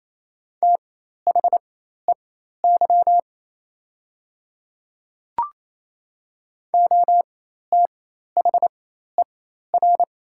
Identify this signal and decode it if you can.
Morse code, heard as a keyed tone, about 28 wpm, spaced as for 13 wpm, 700 Hz. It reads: THEY OTHER
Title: 4x Character Spacing - Top 300 Words - 28wpm